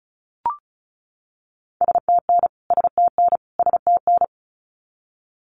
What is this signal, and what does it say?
Morse code, heard as a keyed tone, about 34 wpm, 700 Hz. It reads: STN STN STN
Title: Abbreviations Repeated 3x - 35wpm